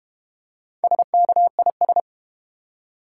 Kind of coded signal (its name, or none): Morse code